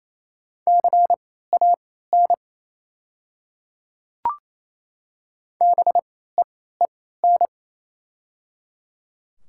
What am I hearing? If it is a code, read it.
Morse code, heard as a keyed tone, about 28 wpm, spaced as for 16 wpm, 700 Hz. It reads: CAN BEEN
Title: 3x Character Spacing - Top 100 Words - 28wpm